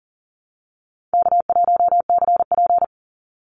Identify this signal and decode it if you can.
Morse code, heard as a keyed tone, about 40 wpm, 700 Hz. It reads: K1CP
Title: Easy Call Signs Repeated 2x - 40wpm